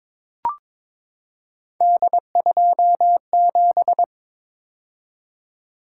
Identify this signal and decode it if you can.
Morse code, heard as a keyed tone, about 23 wpm, 700 Hz. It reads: D27
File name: practice / 22wpm / Three Letter-Number Combinations - 22wpm